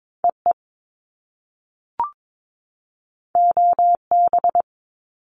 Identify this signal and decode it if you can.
Morse code, heard as a keyed tone, about 22 wpm, 700 Hz. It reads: EE OB
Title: Top Two Letter Combinations in Words 22wpm